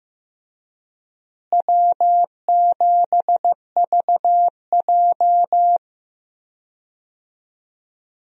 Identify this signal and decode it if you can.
Morse code, heard as a keyed tone, about 15 wpm, 700 Hz. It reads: W7VJ
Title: USA Call Signs - 4 Characters - Repeated 2x - 15wpm